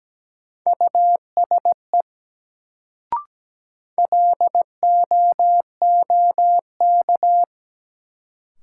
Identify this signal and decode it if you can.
Morse code, heard as a keyed tone, about 17 wpm, 700 Hz. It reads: USE LOOK
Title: Sets of 2 Words - New Word Drill - 17wpm